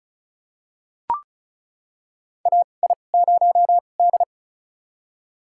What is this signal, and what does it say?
Morse code, heard as a keyed tone, about 34 wpm, spaced as for 25 wpm, 700 Hz. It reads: AI0D
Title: USA Call Signs - 4 Characters - 2x Character Spacing - 35wpm